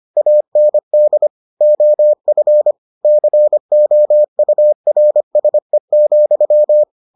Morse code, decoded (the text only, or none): AND OF COURSE,